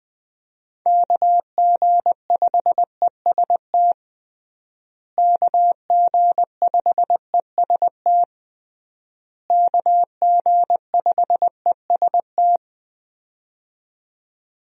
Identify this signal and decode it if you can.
Morse code, heard as a keyed tone, about 20 wpm, 700 Hz. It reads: KG5EST KG5EST KG5EST
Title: USA Call Signs - 6 Characters - Repeated 3x - 20wpm